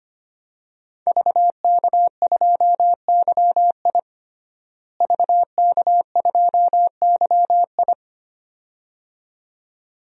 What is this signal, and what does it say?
Morse code, heard as a keyed tone, about 25 wpm, 700 Hz. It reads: VK2YI VK2YI
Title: DXCC-100 Call Signs - 5 Characters - Repeated 2x - 25wpm